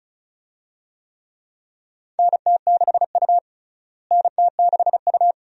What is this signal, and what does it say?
Morse code, heard as a keyed tone, about 36 wpm, 700 Hz. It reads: NT6U NT6U
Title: USA Call Signs - 4 Characters - Repeated 2x - 35wpm